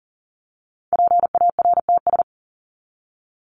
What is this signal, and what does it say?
Morse code, heard as a keyed tone, about 40 wpm, 700 Hz. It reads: PARTS
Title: ES:S Suffix - Top 100 Words with Base Words 40wpm